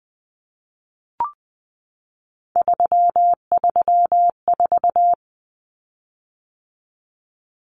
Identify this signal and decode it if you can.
Morse code, heard as a keyed tone, about 20 wpm, 700 Hz. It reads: 334
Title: Common Cut Numbers - 3-Digit 20wpm